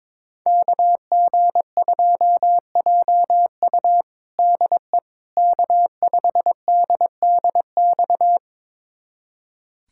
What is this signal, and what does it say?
Morse code, heard as a keyed tone, about 22 wpm, 700 Hz. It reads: KG2JU DE K5DDX
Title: USA Callsign DE Callsign - 5 Characters - 22wpm